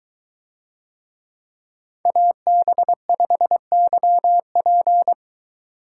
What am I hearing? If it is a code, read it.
Morse code, heard as a keyed tone, about 21 wpm, 700 Hz. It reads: AB5YP